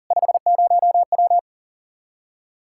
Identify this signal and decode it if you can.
Morse code, heard as a keyed tone, about 51 wpm, 700 Hz. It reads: 50W